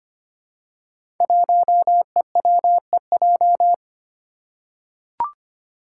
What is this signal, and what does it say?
Morse code, heard as a keyed tone, about 25 wpm, 700 Hz. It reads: 1EWEJ
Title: Five Challenging Letter-Number Combinations 25wpm